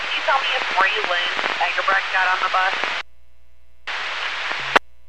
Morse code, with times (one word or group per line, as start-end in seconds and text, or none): none